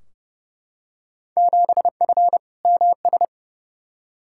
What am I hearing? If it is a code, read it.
Morse code, heard as a keyed tone, about 31 wpm, 700 Hz. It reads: 7F MS
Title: Field Day Exchanges Encore - 30wpm